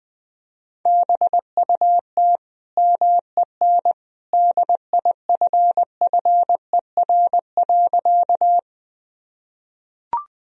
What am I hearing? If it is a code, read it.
Morse code, heard as a keyed tone, about 20 wpm, 700 Hz. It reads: BUT MEN DIFFER.
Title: Sentences from Top 1000 Words - Review Part 1 - 20wpm